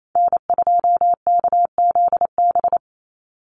Morse code, heard as a keyed tone, about 28 wpm, 700 Hz. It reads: N2KZB
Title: USA Call Signs 28wpm